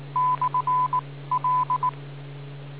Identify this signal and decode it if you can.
Morse code, heard as a keyed tone, about 19 wpm, 1000 Hz. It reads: / L